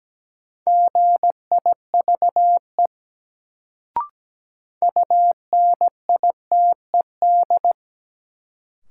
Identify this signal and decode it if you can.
Morse code, heard as a keyed tone, about 17 wpm, 700 Hz. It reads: GIVE UNITED
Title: Sets of 2 Words - New Word Drill - 17wpm